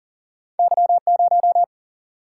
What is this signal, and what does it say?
Morse code, heard as a keyed tone, about 40 wpm, 700 Hz. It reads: Y0
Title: Single-Letter Single-Number 40wpm